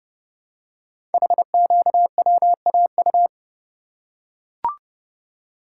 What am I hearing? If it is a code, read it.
Morse code, heard as a keyed tone, about 31 wpm, 690 Hz. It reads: HQWAU
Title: Code Groups - 5 Letters - 30wpm